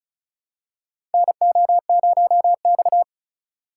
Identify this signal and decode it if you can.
Morse code, heard as a keyed tone, about 34 wpm, 700 Hz. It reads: NO0X
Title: USA Call Signs - 4 Characters - 4x Character Spacing - 35wpm